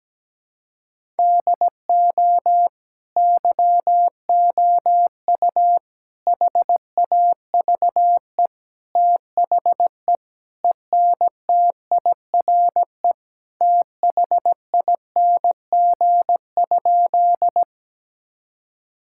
Morse code, heard as a keyed tone, about 17 wpm, 700 Hz. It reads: DO YOU HAVE THE ENTIRE THING?
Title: Sentences from Top 1000 Words - Review Part 5 - 17wpm